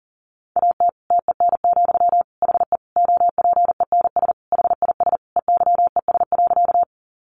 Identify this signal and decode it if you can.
Morse code, heard as a keyed tone, about 40 wpm, 700 Hz. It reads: AT TEN, HE OPENS HIS EYES.
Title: Easy Phrases 40wpm